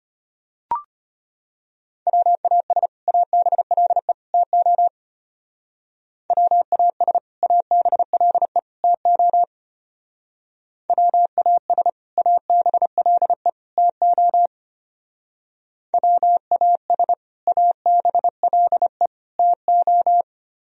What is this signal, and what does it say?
Morse code, heard as a keyed tone, about 30 wpm, 700 Hz. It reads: WAS ABLE TO WAS ABLE TO WAS ABLE TO WAS ABLE TO
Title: Speed-Racing - Sets of 3 Words 25wpm